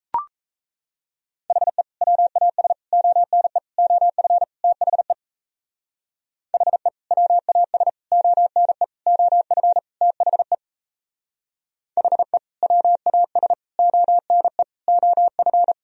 Morse code, heard as a keyed tone, about 37 wpm, 700 Hz. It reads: HE WAS ONE OF THE HE WAS ONE OF THE HE WAS ONE OF